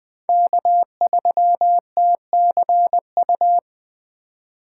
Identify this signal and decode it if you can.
Morse code, heard as a keyed tone, about 20 wpm, 700 Hz. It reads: K3TCU
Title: 2023 SOTA Chaser Call Signs - 20wpm